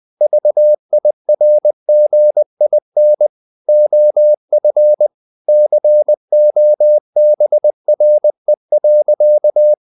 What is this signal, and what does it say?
Morse code, heard as a keyed tone, about 20 wpm, 600 Hz. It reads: VIRGIN OF COBRE.
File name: the old man and the sea / the old man and the sea part 1 20wpm